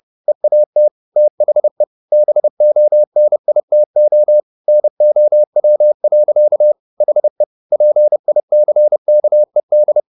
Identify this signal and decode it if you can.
Morse code, heard as a keyed tone, about 30 wpm, 600 Hz. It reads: EAT THE BONITO NOW. HE PICKED
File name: the old man and the sea / the old man and the sea 1 30wpm